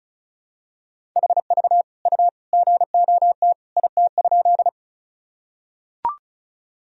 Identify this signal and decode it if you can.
Morse code, heard as a keyed tone, about 36 wpm, 700 Hz. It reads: HV U GOT IT?
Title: Sentences from Top 300 Words and Abbreviations 35wpm